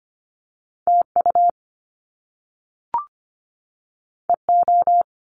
Morse code, heard as a keyed tone, about 25 wpm, 700 Hz. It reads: TU EO
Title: Top Two Letter Combinations in Words 25wpm